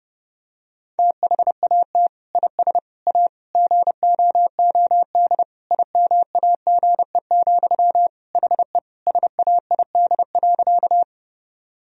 Morse code, heard as a keyed tone, about 30 wpm, 700 Hz. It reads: THAT IS A GOOD IMAGE, HE SAID.
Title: Sentences from Top 1000 Words - Review Part 4 - 30wpm